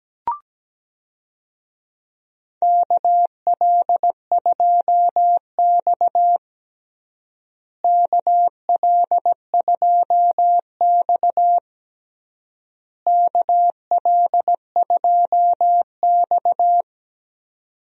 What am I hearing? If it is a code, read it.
Morse code, heard as a keyed tone, about 17 wpm, 700 Hz. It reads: KL2X KL2X KL2X
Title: USA Call Signs - 4 Characters - Repeated 3x - 17wpm